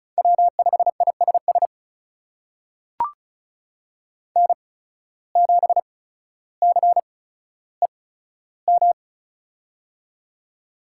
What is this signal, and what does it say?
Morse code, heard as a keyed tone, about 36 wpm, spaced as for 15 wpm, 700 Hz. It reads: W5ISS N7CEM
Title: USA Call Signs - 5 Characters - 8x Character Spacing - 35wpm